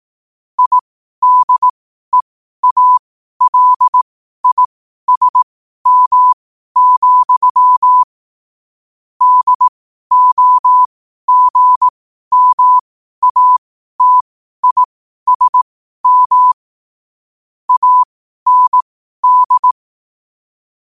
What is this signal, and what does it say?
Morse code, heard as a keyed tone, about 18 wpm, spaced as for 12 wpm, 1000 Hz. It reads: IDEALISM, DOGMATISM AND
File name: qotdfiles / 12wpm